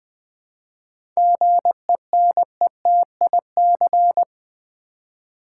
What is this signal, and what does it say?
Morse code, heard as a keyed tone, about 20 wpm, 700 Hz. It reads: GENETIC
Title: 7 Letter Words - First 3 Letters Given - 20wpm